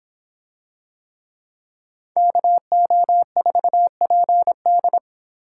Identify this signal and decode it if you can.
Morse code, heard as a keyed tone, about 26 wpm, 700 Hz. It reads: KO4PD